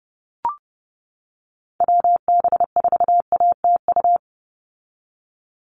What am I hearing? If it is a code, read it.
Morse code, heard as a keyed tone, about 30 wpm, 700 Hz. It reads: WB4ATU
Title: USA Call Signs 30wpm